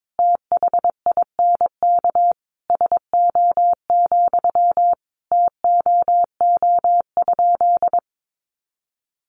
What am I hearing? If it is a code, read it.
Morse code, heard as a keyed tone, about 22 wpm, 700 Hz. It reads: THINK SO, TOO?
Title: Sentences from Top 200 Words 22wpm